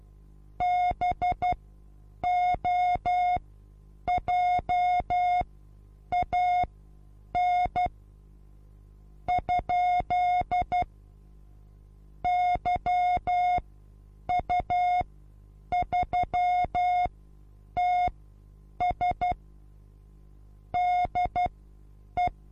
Morse code, heard as a keyed tone, about 12 wpm, spaced as for 8 wpm, 720 Hz. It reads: BOJAN ? YU3TS DE
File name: Onlinelehrgang / morsen / DJ4UF Morsen23